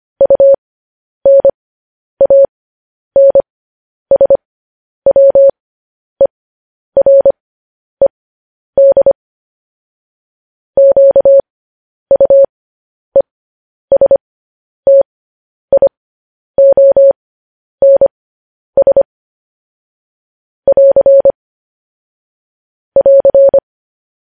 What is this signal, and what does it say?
Morse code, heard as a keyed tone, about 25 wpm, spaced as for 10 wpm, 550 Hz. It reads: UNANSWERED QUESTIONS <AR> <AR>
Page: news headlines in morse code at 10 wpm